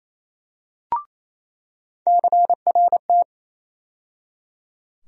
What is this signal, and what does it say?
Morse code, heard as a keyed tone, about 28 wpm, 700 Hz. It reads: CRT
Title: USA Call Sign Postfixes - 3 Letters 28wpm